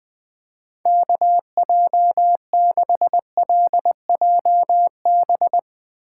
Morse code, heard as a keyed tone, about 20 wpm, 700 Hz. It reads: KJ6LJB
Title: USA Call Signs - 6 Characters - 8x Character Spacing - 20wpm